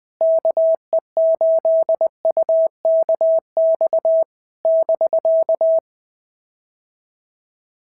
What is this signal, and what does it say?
Morse code, heard as a keyed tone, about 20 wpm, 650 Hz. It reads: KE8UKX <BK>
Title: POTA QSO Practice - 20wpm